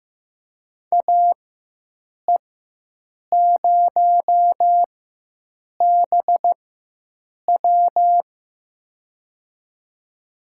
Morse code, heard as a keyed tone, about 15 wpm, spaced as for 7 wpm, 700 Hz. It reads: AE0BW
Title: USA Call Signs - 4x Character Spacing - 15wpm